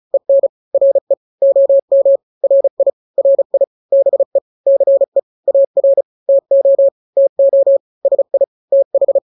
Morse code, heard as a keyed tone, about 35 wpm, 550 Hz. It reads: EN RE OM RI RI BE CE AR TO TO SI TH
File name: n-grams / 2-grams 35 30 WPM